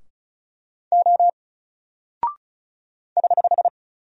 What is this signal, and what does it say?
Morse code, heard as a keyed tone, about 34 wpm, 700 Hz. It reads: O <HH>
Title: Single Character (Letter:Number:Punctuation:Prosign) 35wpm